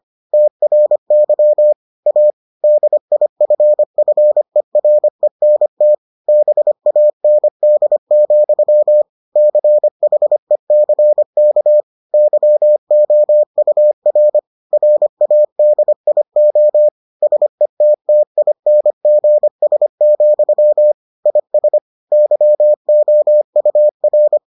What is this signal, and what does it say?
Morse code, heard as a keyed tone, about 25 wpm, 600 Hz. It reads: TRY A DIFFERENT BAND, CHECK YOUR RADIO SETTINGS, IS YOUR